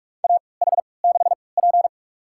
Morse code, heard as a keyed tone, about 44 wpm, spaced as for 25 wpm, 690 Hz. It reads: AH6P
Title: USA Call Signs - 4 Characters - 3x Character Spacing - 45wpm